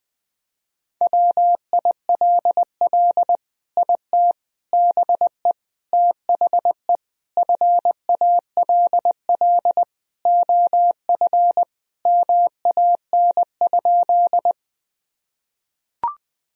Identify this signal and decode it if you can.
Morse code, heard as a keyed tone, about 21 wpm, 700 Hz. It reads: WILL IT BE THE FALL OF MAN?
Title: Sentences from Top 1000 Words - Review Part 2 - 20wpm